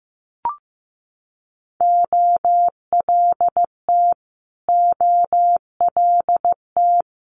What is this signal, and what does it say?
Morse code, heard as a keyed tone, about 15 wpm, 700 Hz. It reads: OLT OLT
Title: Multi-letter Phonemes Repeated 3x 15wpm